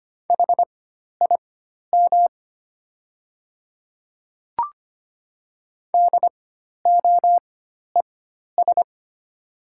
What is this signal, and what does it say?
Morse code, heard as a keyed tone, about 25 wpm, spaced as for 12 wpm, 700 Hz. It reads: HIM DOES
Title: 4x Character Spacing - Top 200 Words - 25wpm